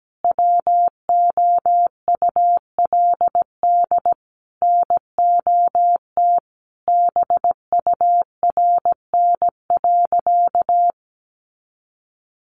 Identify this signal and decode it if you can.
Morse code, heard as a keyed tone, about 17 wpm, 700 Hz. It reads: WOULD NOT BURN.
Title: Sentences from Top 1300 Words 17wpm